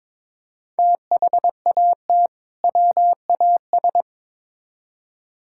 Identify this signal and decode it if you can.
Morse code, heard as a keyed tone, about 22 wpm, 700 Hz. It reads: THAT WAS
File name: practice / 22wpm / Build A Phrase - 2-Word Phrases - 22wpm